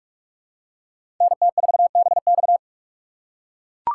Morse code, heard as a keyed tone, about 42 wpm, 700 Hz. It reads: NT4BX